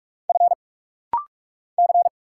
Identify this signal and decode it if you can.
Morse code, heard as a keyed tone, about 46 wpm, 700 Hz. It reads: F C
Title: Single Letters - Rapid-Fire 45wpm